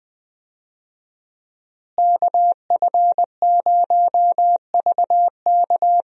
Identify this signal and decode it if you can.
Morse code, heard as a keyed tone, about 20 wpm, 700 Hz. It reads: KF0VK
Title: USA Call Signs - 5 Characters - Repeated 3x - 20wpm